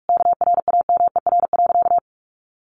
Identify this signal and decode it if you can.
Morse code, heard as a keyed tone, about 44 wpm, 700 Hz. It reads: KRAMER.
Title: A Sound Of Thunder 45wpm